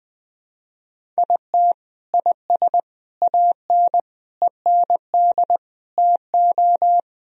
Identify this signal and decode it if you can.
Morse code, heard as a keyed tone, about 20 wpm, 700 Hz. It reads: IT IS AN END TO
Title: Sentences from Top 1000 Words - Review Part 1 - 20wpm